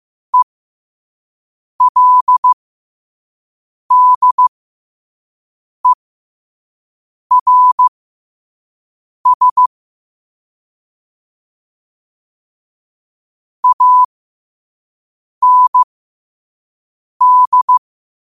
Morse code, heard as a keyed tone, about 15 wpm, spaced as for 5 wpm, 1000 Hz. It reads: ELDERS AND
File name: qotdfiles / 5wpm